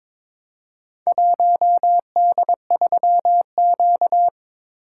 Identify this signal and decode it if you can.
Morse code, heard as a keyed tone, about 22 wpm, 700 Hz. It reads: 1D3Q
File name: practice / 22wpm / Four Letter-Number Combinations - 22wpm